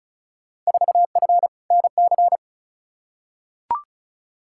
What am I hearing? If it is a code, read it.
Morse code, heard as a keyed tone, about 36 wpm, 700 Hz. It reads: 4F NC
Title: Field Day Exchanges - 35wpm